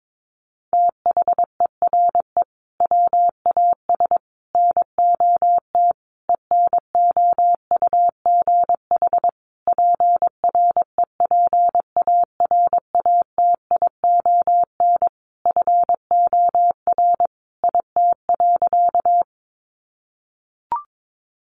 Morse code, heard as a keyed tone, about 22 wpm, 700 Hz. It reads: THERE WAS NOT ENOUGH PREPARATION FOR IT.
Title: Sentences from Top 1400 Words 22wpm